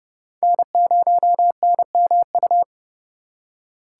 Morse code, heard as a keyed tone, about 30 wpm, 700 Hz. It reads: N0NMU
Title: USA Call Signs - 5 Characters - 4x Character Spacing - 30wpm